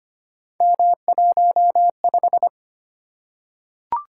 Morse code, heard as a keyed tone, about 25 wpm, 700 Hz. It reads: M15
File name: Three Letter-Number Combinations - 25wpm